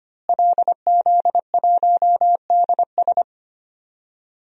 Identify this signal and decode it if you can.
Morse code, heard as a keyed tone, about 25 wpm, 700 Hz. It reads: LZ1DS
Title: DXCC-100 Call Signs - 5 Characters - Repeated 2x - 25wpm